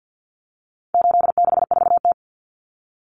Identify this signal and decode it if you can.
Morse code, heard as a keyed tone, about 50 wpm, 700 Hz. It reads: 864T
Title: Common Cut Numbers - 4-Digit 50wpm